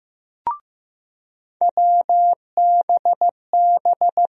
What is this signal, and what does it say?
Morse code, heard as a keyed tone, about 15 wpm, 700 Hz. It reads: WBB
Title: USA Call Sign Postfixes - 3 Letters 15wpm